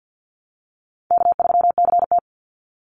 Morse code, heard as a keyed tone, about 50 wpm, 700 Hz. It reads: K3CT
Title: Easy Call Signs Repeated 3x - 50wpm